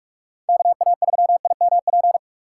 Morse code, heard as a keyed tone, about 43 wpm, 700 Hz. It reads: K A 3 I M P
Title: USA Call Signs - 6 Characters - 6x Character Spacing - 45wpm